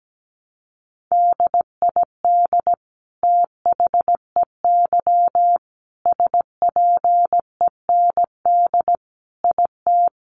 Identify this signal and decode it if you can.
Morse code, heard as a keyed tone, about 17 wpm, 700 Hz. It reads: DID THEY SPEND IT
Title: Sentences from Top 800 Words 17wpm